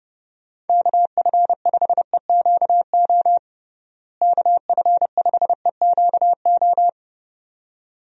Morse code, heard as a keyed tone, about 30 wpm, 700 Hz. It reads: KF5EQO KF5EQO
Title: USA Call Signs - 6 Characters - Repeated 2x - 30wpm